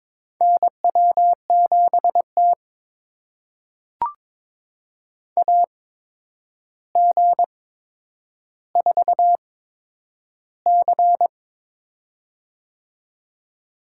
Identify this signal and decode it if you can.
Morse code, heard as a keyed tone, about 22 wpm, spaced as for 9 wpm, 700 Hz. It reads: NW7T AG4C
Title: USA Call Signs - 4 Characters - 8x Character Spacing - 22wpm